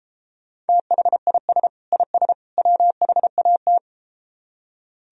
Morse code, heard as a keyed tone, about 33 wpm, 700 Hz. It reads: THIS IS WHAT